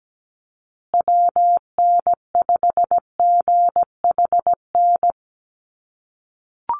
Morse code, heard as a keyed tone, about 17 wpm, 700 Hz. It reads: WN5GHN
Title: USA Call Signs Repeated 2x 17wpm